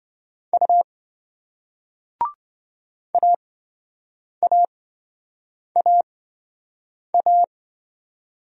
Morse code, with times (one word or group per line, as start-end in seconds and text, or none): 0.5-0.8 U
3.1-7.4 AAAA